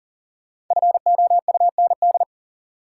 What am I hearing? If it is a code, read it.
Morse code, heard as a keyed tone, about 40 wpm, 700 Hz. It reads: FOUND
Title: Sentences from Top 300 Words - New Word Drill - 40wpm